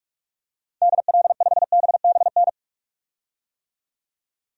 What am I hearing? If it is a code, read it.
Morse code, heard as a keyed tone, about 46 wpm, 700 Hz. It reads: DL5BBN
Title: DXCC-100 Call Signs - 6 Characters - 45wpm